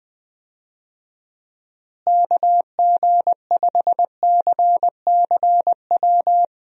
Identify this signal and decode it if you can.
Morse code, heard as a keyed tone, about 19 wpm, 700 Hz. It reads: KG5CCW
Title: USA Call Signs - 6 Characters - Repeated 3x - 20wpm